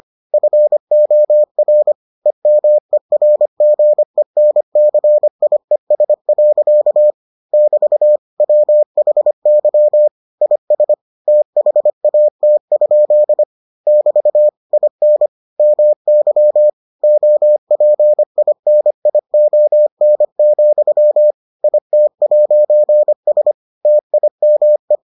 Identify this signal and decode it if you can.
Morse code, heard as a keyed tone, about 25 wpm, 600 Hz. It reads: FOR EMERGENCIES. <BT> WHY IS THAT? <BT> IN MY OPINION, IT'S TIME